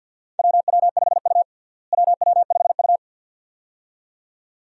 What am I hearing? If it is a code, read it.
Morse code, heard as a keyed tone, about 48 wpm, 700 Hz. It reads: WW5U WW5U
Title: USA Call Signs - 4 Characters - Repeated 2x - 50wpm